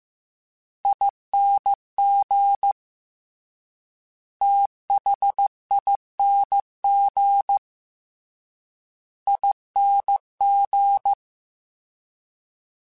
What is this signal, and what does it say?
Morse code, heard as a keyed tone, about 15 wpm, 800 Hz. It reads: ING THING ING
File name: Special INGs-mp3s / Special INGs-mp3s / Fast ing @13